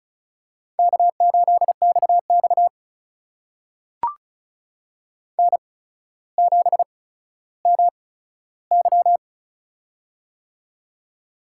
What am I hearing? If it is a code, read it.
Morse code, heard as a keyed tone, about 36 wpm, spaced as for 15 wpm, 700 Hz. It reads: K8XX N7MY
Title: USA Call Signs - 4 Characters - 8x Character Spacing - 35wpm